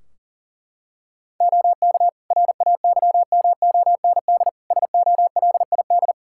Morse code, heard as a keyed tone, about 40 wpm, 700 Hz. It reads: OK RAYMOND SOLID
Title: Practice QSO - Exchange 3 - More Info - 40wpm